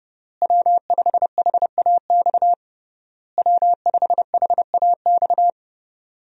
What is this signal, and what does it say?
Morse code, heard as a keyed tone, about 30 wpm, 700 Hz. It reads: W5HAX W5HAX
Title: USA Call Signs - 5 Characters - Repeated 3x - 30wpm